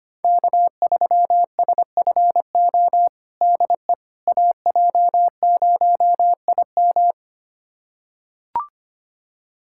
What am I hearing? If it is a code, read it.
Morse code, heard as a keyed tone, about 25 wpm, 700 Hz. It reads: K3SFO DE AJ0IM
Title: USA Callsign DE Callsign - 5 Characters - 25wpm